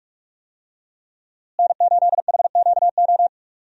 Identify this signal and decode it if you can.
Morse code, heard as a keyed tone, about 47 wpm, 700 Hz. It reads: N8HQO
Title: USA Call Signs - 5 Characters - Repeated 3x - 45wpm